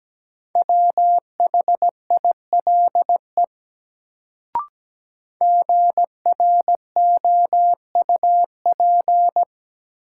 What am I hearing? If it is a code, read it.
Morse code, heard as a keyed tone, about 17 wpm, 700 Hz. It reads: WHILE GROUP
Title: Sets of 3 Words - New Word Drill - 17wpm